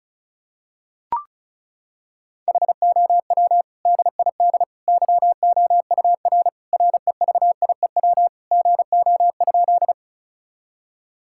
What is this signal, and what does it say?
Morse code, heard as a keyed tone, about 36 wpm, 700 Hz. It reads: HOW DID YOUR REVIEW GO?
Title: Sentences from Top 1000 Words - Review Part 6 - 35wpm